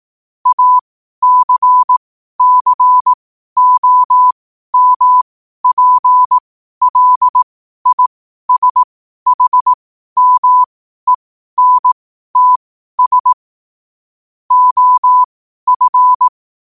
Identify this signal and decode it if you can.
Morse code, heard as a keyed tone, about 18 wpm, spaced as for 12 wpm, 1000 Hz. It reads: ACCOMPLISHMENTS OF
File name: qotdfiles / 12wpm